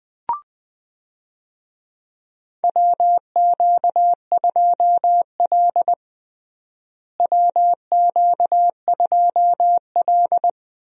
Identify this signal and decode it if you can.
Morse code, heard as a keyed tone, about 20 wpm, 700 Hz. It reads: WQ2L WQ2L
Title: USA Call Signs - 4 Characters - Repeated 2x - 20wpm